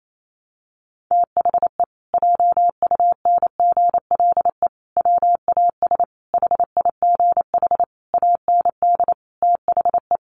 Morse code, heard as a keyed tone, about 28 wpm, 700 Hz. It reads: THE JUNGLE WAS HIGH AND THE
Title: A Sound Of Thunder 28wpm